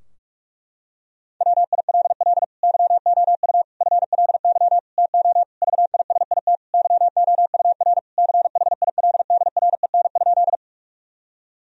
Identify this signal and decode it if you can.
Morse code, heard as a keyed tone, about 46 wpm, 700 Hz. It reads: WILL YOU FLY TO VISIT YOUR CHILDREN?
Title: Sentences from Top 1000 Words - Review Part 2 - 45wpm